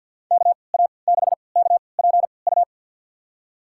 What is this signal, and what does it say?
Morse code, heard as a keyed tone, about 52 wpm, spaced as for 28 wpm, 700 Hz. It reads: KA6KPU
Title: USA Call Signs - 3x Character Spacing - 50wpm